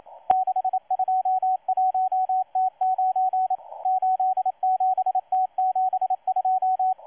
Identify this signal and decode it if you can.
Morse code, heard as a keyed tone, about 27 wpm, 750 Hz, with radio noise. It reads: 621T9 87T72
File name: M12-2016-07-07-1310utc-13926khz-error-restart